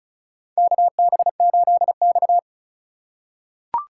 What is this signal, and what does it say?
Morse code, heard as a keyed tone, about 36 wpm, 700 Hz. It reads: KB8X
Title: USA Call Signs - 4 Characters - 2x Character Spacing - 35wpm